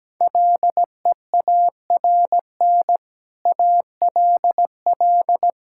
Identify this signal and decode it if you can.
Morse code, heard as a keyed tone, about 17 wpm, 700 Hz. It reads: LEARN ALL
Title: Sentences from Top 1000 Words - Review Part 5 - 17wpm